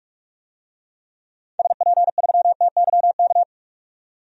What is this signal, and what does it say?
Morse code, heard as a keyed tone, about 45 wpm, 700 Hz. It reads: SP3TYK
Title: DXCC-100 Call Signs - 6 Characters - 45wpm